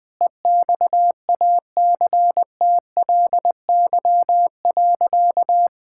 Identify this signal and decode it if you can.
Morse code, heard as a keyed tone, about 19 wpm, 700 Hz. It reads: EXACTLY.
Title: Sentences from Top 1000 Words - Review Part 6 - 20wpm